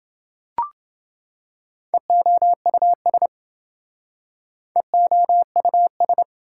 Morse code, heard as a keyed tone, about 27 wpm, 700 Hz. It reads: EOUS EOUS